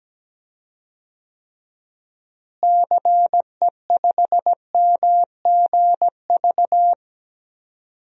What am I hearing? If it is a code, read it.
Morse code, heard as a keyed tone, about 17 wpm, 700 Hz. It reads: CE5MGV
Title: DXCC-100 Call Signs - 6 Characters - Repeated 3x - 17wpm